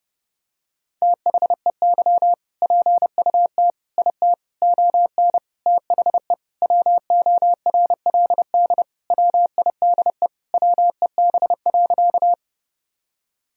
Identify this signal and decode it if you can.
Morse code, heard as a keyed tone, about 30 wpm, 700 Hz. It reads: THEY PUT IT ON THE WORLD WIDE WEB.
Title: Sentences from Top 1000 Words - Review Part 4 - 30wpm